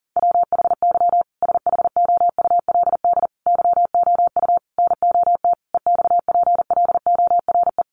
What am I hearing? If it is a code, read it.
Morse code, heard as a keyed tone, about 40 wpm, 700 Hz. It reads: WHY SHOULD YOU NOT EXPLORE
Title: Sentences from Top 1100 Words 40wpm